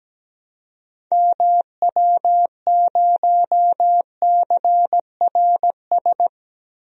Morse code, heard as a keyed tone, about 17 wpm, 700 Hz. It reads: MW0CRS
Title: DXCC-100 Call Signs - 6 Characters - Repeated 2x - 17wpm